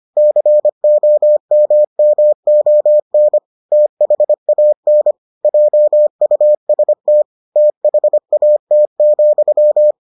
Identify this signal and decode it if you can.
Morse code, heard as a keyed tone, about 25 wpm, 600 Hz. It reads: COMMON THAN JUST THAT,